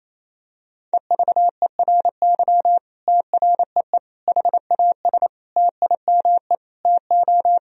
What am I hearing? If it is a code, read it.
Morse code, heard as a keyed tone, about 28 wpm, 700 Hz. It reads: EVERY TREE HAS TIME TO
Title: Sentences from Top 1000 Words - Review Part 1 - 28wpm